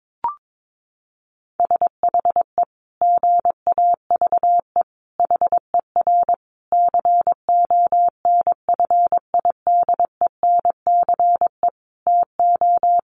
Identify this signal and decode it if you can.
Morse code, heard as a keyed tone, about 22 wpm, 700 Hz. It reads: SHE GAVE HER CONFIDENCE TO
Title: Sentences from Top 1500 Words 22wpm